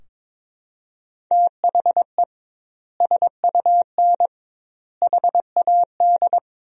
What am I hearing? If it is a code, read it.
Morse code, heard as a keyed tone, about 22 wpm, 700 Hz. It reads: THE SUN HAD
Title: 2x Word-Spacing - Sentences from Top 300 Words - 22wpm